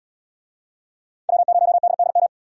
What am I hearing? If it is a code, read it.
Morse code, heard as a keyed tone, about 70 wpm, 700 Hz. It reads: W0DDM